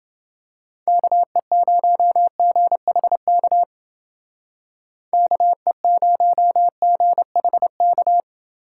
Morse code, heard as a keyed tone, about 29 wpm, 700 Hz. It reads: KE0GHK KE0GHK